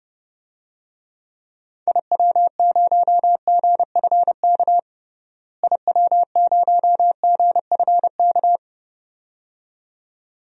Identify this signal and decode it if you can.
Morse code, heard as a keyed tone, about 32 wpm, 700 Hz. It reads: IW0GFK IW0GFK
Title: DXCC-100 Call Signs - 6 Characters - Repeated 2x - 30wpm